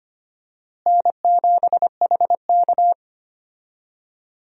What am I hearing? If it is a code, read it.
Morse code, heard as a keyed tone, about 25 wpm, 700 Hz. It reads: N7HK
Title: USA Call Signs - 4 Characters - Repeated 2x - 25wpm